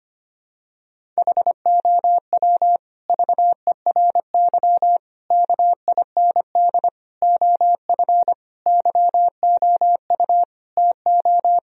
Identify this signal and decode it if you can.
Morse code, heard as a keyed tone, about 25 wpm, 700 Hz. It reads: HOW VERY KIND OF YOU TO
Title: Sentences from Top 1000 Words - Review Part 1 - 25wpm